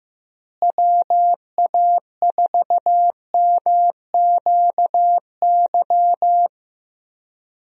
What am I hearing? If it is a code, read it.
Morse code, heard as a keyed tone, about 15 wpm, 700 Hz. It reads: WA4MQY
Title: USA Call Signs - 6 Characters - Repeated 3x - 15wpm